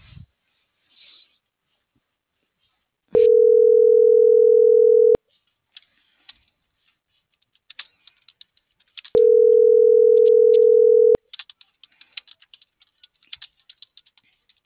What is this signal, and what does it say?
Ringback tone (North America), 440 + 480 Hz.